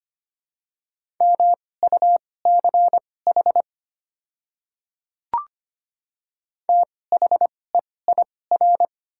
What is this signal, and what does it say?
Morse code, heard as a keyed tone, about 25 wpm, spaced as for 18 wpm, 700 Hz. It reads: MUCH THEIR
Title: 2x Character Spacing - Top 200 Words - 25wpm